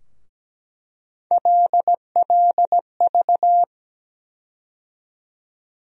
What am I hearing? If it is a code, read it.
Morse code, heard as a keyed tone, about 17 wpm, 700 Hz. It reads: LLV
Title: USA Call Sign Postfixes - 3 Letters 17wpm